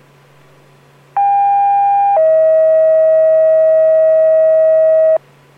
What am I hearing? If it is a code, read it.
Two-tone page, heard as a one-second tone, then a three-second tone, about 800 then 640 Hz. It is heard from 1.2 to 5.2 s.